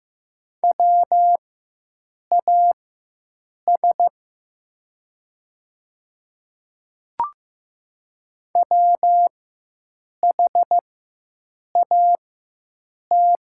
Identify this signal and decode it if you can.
Morse code, heard as a keyed tone, about 15 wpm, spaced as for 7 wpm, 700 Hz. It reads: WAS WHAT
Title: 4x Character Spacing - Top 100 Words - 15wpm